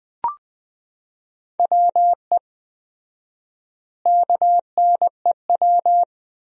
Morse code, heard as a keyed tone, about 21 wpm, 700 Hz. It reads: WE KNEW
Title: 4x Word-Spacing - Sentences from Top 400 Words - 20wpm